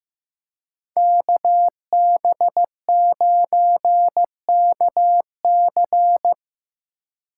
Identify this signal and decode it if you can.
Morse code, heard as a keyed tone, about 18 wpm, 700 Hz. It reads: KB9KC
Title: Speed Racing - USA 2x2 Call Signs 15wpm